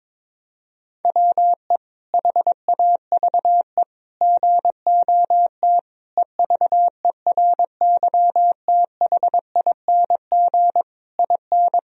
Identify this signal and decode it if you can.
Morse code, heard as a keyed tone, about 22 wpm, 700 Hz. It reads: WE HAVE GOT EVERYTHING IN
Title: Sentences from Top 1000 Words - Review Part 3 - 22wpm